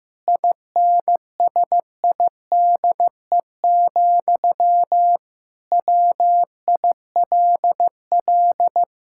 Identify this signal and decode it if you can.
Morse code, heard as a keyed tone, about 16 wpm, 700 Hz. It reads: INSIDE, WILL
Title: Sentences from Top 1000 Words - Review Part 6 - 15wpm